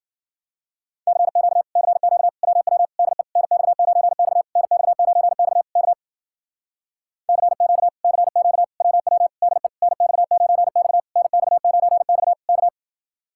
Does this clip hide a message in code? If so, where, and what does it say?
Yes: Morse code from 1.1 to 12.7 s, heard as a keyed tone, about 52 wpm, 700 Hz. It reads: CQ CQ WW DE NX9X NX9X K CQ CQ WW DE NX9X NX9X K